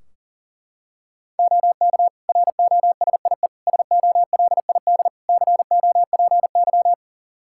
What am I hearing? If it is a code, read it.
Morse code, heard as a keyed tone, about 40 wpm, 700 Hz. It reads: OK ROSIE SOLID COPY
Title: Practice QSO - Exchange 3 - More Info - 40wpm